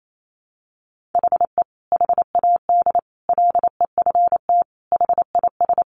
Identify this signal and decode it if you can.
Morse code, heard as a keyed tone, about 28 wpm, 700 Hz. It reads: HE HAD LEFT HIS
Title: Sentences from Top 1500 Words 28wpm